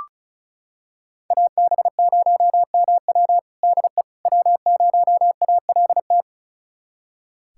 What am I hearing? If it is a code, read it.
Morse code, heard as a keyed tone, about 36 wpm, 700 Hz. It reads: AB0MW DE W0ALT
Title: USA Callsign DE Callsign - 5 Characters - 35wpm